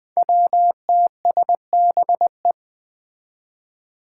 Morse code, heard as a keyed tone, about 20 wpm, 700 Hz. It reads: WTSBE
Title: Five Challenging Letter-Number Combinations 20wpm